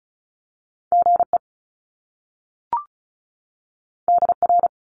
Morse code, heard as a keyed tone, about 36 wpm, 700 Hz. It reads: GE DR
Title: Top Two Letter Combinations in Words 35wpm